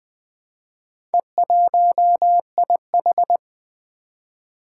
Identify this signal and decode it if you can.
Morse code, heard as a keyed tone, about 20 wpm, 700 Hz. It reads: E1IH
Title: Four Challenging Letter-Number Combinations 20wpm